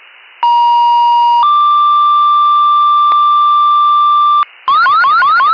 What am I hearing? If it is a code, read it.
Two-tone page, heard as a one-second tone, then a three-second tone, about 950 then 1180 Hz, with noise. It is heard from 0.4 to 4.4 s.